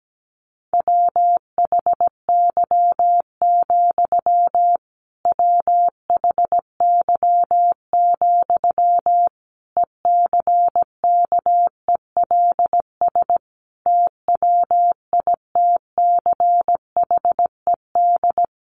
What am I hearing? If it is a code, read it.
Morse code, heard as a keyed tone, about 17 wpm, 700 Hz. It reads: WHY, WHY, ECKELS TWITCHED
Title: A Sound Of Thunder 17wpm